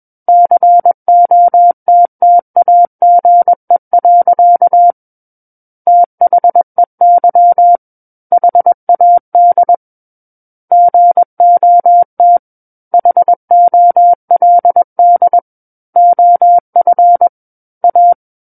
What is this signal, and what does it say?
Morse code, heard as a keyed tone, about 21 wpm, 700 Hz. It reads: COTTAGE. THEY HAD GOT HOLD OF A
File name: Book One / 12-Chapter 12 - 21 WPM